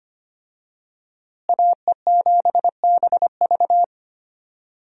Morse code, heard as a keyed tone, about 25 wpm, 700 Hz. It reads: AE7BV
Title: USA Call Signs - 5 Characters - Repeated 3x - 25wpm